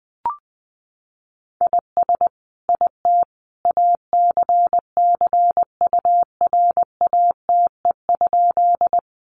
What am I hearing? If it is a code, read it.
Morse code, heard as a keyed tone, about 20 wpm, 700 Hz. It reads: IS IT ACCURATE?
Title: Sentences from Top 1200 Words 20wpm